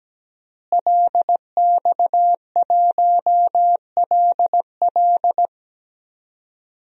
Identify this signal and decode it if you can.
Morse code, heard as a keyed tone, about 17 wpm, 700 Hz. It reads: LX1LL
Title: DXCC-100 Call Signs - 5 Characters - Repeated 2x - 17wpm